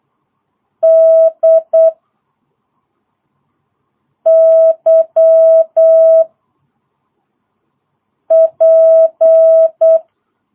Morse code, heard as a keyed tone, about 8 wpm, 660 Hz. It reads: D Y P
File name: jh8cbh / 2022cw